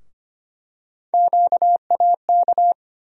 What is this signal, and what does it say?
Morse code, heard as a keyed tone, about 25 wpm, 700 Hz. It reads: QAK
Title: USA Call Sign Postfixes - 3 Letters 25wpm